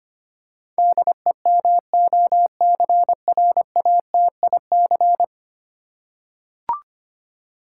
Morse code, heard as a keyed tone, about 25 wpm, 700 Hz. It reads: DEMOCRATIC
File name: 10 Letter Words - First 3 Letters Given - 25wpm